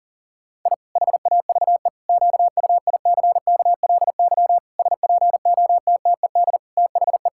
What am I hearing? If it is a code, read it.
Morse code, heard as a keyed tone, about 40 wpm, 700 Hz. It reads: I HAVE QUICKLY SPOTTED THE
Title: Pangram Challenge v2 - 40wpm